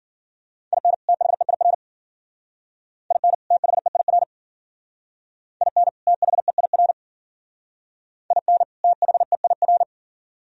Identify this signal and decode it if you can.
Morse code, heard as a keyed tone, about 52 wpm, 700 Hz. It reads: IN THEIR IN THEIR IN THEIR IN THEIR